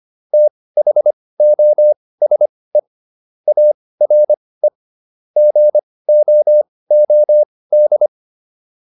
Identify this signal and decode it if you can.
Morse code, heard as a keyed tone, about 25 wpm, spaced as for 18 wpm, 600 Hz. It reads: THOSE ARE GOOD